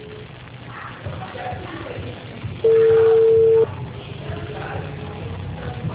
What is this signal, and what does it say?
Ringback tone (China, 450 Hz).